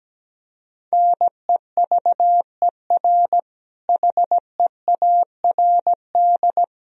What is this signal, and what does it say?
Morse code, heard as a keyed tone, about 17 wpm, 700 Hz. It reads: NEVER HEARD
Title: Sentences from Top 400 Words and Abbreviations 17wpm